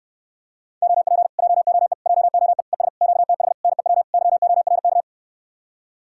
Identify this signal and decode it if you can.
Morse code, heard as a keyed tone, about 74 wpm, 700 Hz. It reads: OP JOE JOE ES QTH NEW YORK